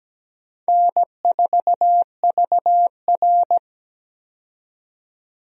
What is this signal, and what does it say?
Morse code, heard as a keyed tone, about 17 wpm, 700 Hz. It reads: N4VR
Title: USA Call Signs - 4 Characters - 17wpm